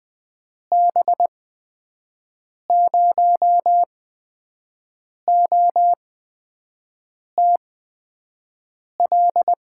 Morse code, heard as a keyed tone, about 20 wpm, spaced as for 5 wpm, 700 Hz. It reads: B0OTL
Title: USA Call Signs - 8x Character Spacing - 20wpm